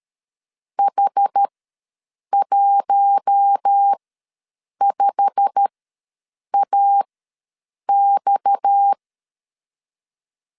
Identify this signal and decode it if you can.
Morse code, heard as a keyed tone, about 13 wpm, spaced as for 7 wpm, 800 Hz. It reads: H15AX
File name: LEZIONI / 9 286 ws 12wpm 3351